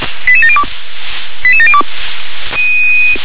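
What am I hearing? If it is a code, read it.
Five-tone selective call (ZVEI), heard as quick runs of five tones, tones of about 80 ms. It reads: 88702 88702
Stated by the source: ZVEI five-tone selective call received with an SDR.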